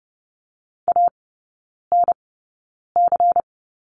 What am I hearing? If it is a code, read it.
Morse code, heard as a keyed tone, about 30 wpm, spaced as for 9 wpm, 700 Hz. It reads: ANC
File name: Lesson 17e - Letter u - Three Characters 30wpm